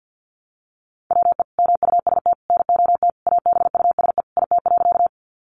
Morse code, heard as a keyed tone, about 50 wpm, 700 Hz. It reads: WE MUST NOT ABUSE IT.
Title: Sentences from Top 1300 Words 50wpm